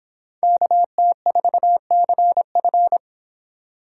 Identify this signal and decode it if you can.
Morse code, heard as a keyed tone, about 26 wpm, 700 Hz. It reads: KT4CF